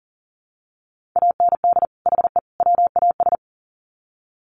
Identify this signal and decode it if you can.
Morse code, heard as a keyed tone, about 40 wpm, 700 Hz. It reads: AND HE WAS
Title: Sets of 3 Words 40wpm